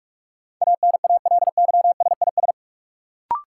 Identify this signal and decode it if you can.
Morse code, heard as a keyed tone, about 46 wpm, 700 Hz. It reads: ANALYSIS
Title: Sentences from Top 1000 Words - Review Part 5 - New Word Drill - 45wpm